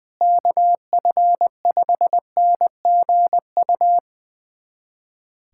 Morse code, heard as a keyed tone, about 20 wpm, 700 Hz. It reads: KF5NGU
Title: USA Callsign DE Callsign - 6 Characters - 20wpm